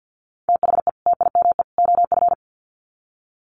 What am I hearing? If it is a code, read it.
Morse code, heard as a keyed tone, about 50 wpm, 700 Hz. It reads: THE TIME OF
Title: Sets of 3 Words 50wpm